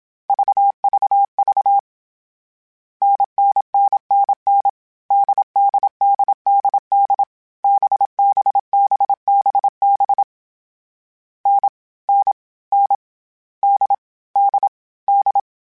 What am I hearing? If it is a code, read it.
Morse code, heard as a keyed tone, about 27 wpm, 800 Hz. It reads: VVV NNNNN DDDDD BBBBB N N N D D D
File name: Difficult Combos / Difficult Combos-mp3s / NDB NDB6 @25